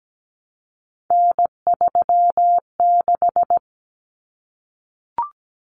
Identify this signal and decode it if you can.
Morse code, heard as a keyed tone, about 17 wpm, 700 Hz. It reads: N36
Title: Common Cut Numbers - 3-Digit 17wpm